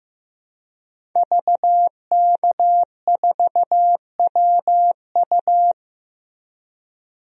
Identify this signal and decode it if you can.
Morse code, heard as a keyed tone, about 15 wpm, 700 Hz. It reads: VK4WU
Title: DXCC-100 Call Signs - 5 Characters - Repeated 3x - 15wpm